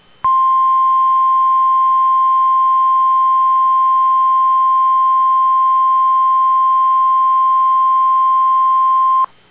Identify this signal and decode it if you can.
Emergency Alert System (SAME) tones (the 1050 Hz attention tone). It is heard from 0.2 to 9.3 s.